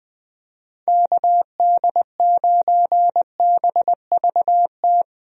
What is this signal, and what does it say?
Morse code, heard as a keyed tone, about 20 wpm, 700 Hz. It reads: KD9BVT
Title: USA Call Signs - 6 Characters - Repeated 2x - 20wpm